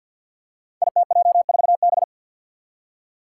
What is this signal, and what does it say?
Morse code, heard as a keyed tone, about 46 wpm, 700 Hz. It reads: ITJ4B
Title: Five Challenging Letter-Number Combinations 50wpm